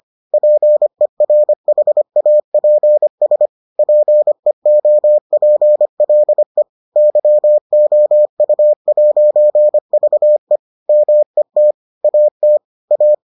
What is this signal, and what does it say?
Morse code, heard as a keyed tone, about 25 wpm, 600 Hz. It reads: PERHAPS PEOPLE YOU'VE MET AT A